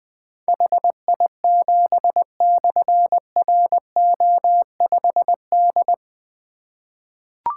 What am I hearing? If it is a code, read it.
Morse code, heard as a keyed tone, about 20 wpm, 700 Hz. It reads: HI7/RO5D
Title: Difficult Call Signs 20wpm